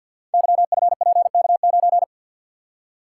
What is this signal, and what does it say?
Morse code, heard as a keyed tone, about 52 wpm, 700 Hz. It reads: YFPK9
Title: Code Groups - 5 Letter:Numbers - 50wpm